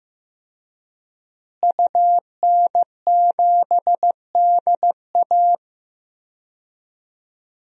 Morse code, heard as a keyed tone, about 15 wpm, 700 Hz. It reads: UN7DA
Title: DXCC-100 Call Signs - 5 Characters - Repeated 2x - 15wpm